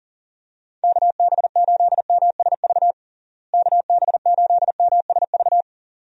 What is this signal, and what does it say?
Morse code, heard as a keyed tone, about 40 wpm, 700 Hz. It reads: KB8MSV KB8MSV
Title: USA Call Signs - 6 Characters - Repeated 2x - 40wpm